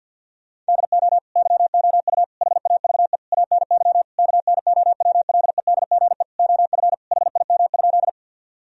Morse code, heard as a keyed tone, about 50 wpm, 700 Hz. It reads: DO YOU HAVE ANY KNOWLEDGE OF HIM?
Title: Sentences from Top 1000 Words - Review Part 3 - 50wpm